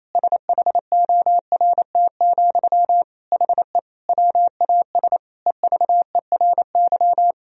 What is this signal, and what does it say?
Morse code, heard as a keyed tone, about 28 wpm, 700 Hz. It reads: SHORT, HE WAS EVERY
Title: Sentences from Top 1000 Words - Review Part 3 - 28wpm